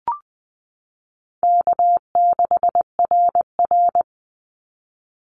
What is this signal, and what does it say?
Morse code, heard as a keyed tone, about 20 wpm, 700 Hz. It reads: K6RR
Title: Call Signs Repeated 3x - 20wpm